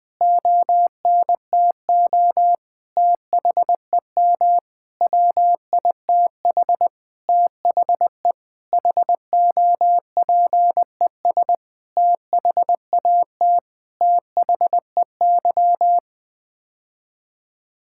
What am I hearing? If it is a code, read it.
Morse code, heard as a keyed tone, about 20 wpm, 700 Hz. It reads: ONTO THEM WITH THE HOPES THAT THEY
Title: Sets of 9 Words - 20wpm